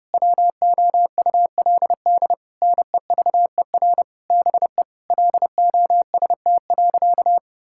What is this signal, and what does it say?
Morse code, heard as a keyed tone, about 32 wpm, 700 Hz. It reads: WOULD NEVER BE LOST.
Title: Sentences from Top 1600 Words 30wpm